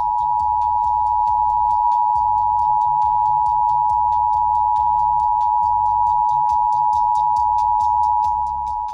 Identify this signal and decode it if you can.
Emergency Alert System (SAME) tones (the 853 and 960 Hz attention tone). It is heard from 0.4 to 8.3 s.